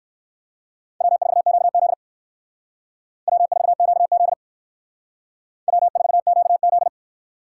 Morse code, heard as a keyed tone, about 57 wpm, 700 Hz. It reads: W4QZ W4QZ W4QZ